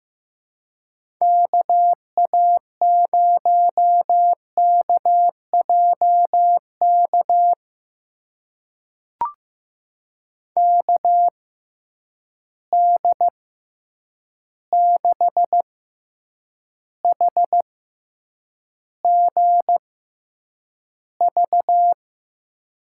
Morse code, heard as a keyed tone, about 15 wpm, spaced as for 8 wpm, 700 Hz. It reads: KA0KJK KD6HGV
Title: USA Call Signs - 6 Characters - 6x Character Spacing - 15wpm